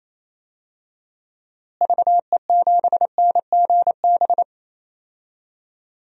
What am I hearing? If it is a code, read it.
Morse code, heard as a keyed tone, about 28 wpm, 700 Hz. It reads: VE7NGB
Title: DXCC-100 Call Signs - 6 Characters - Repeated 2x - 28wpm